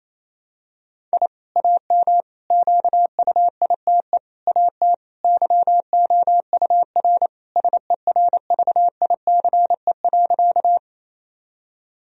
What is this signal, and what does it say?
Morse code, heard as a keyed tone, about 27 wpm, 700 Hz. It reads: I AM QUITE AT YOUR SERVICE.
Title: Sentences from Top 1000 Words - Review Part 4 - 28wpm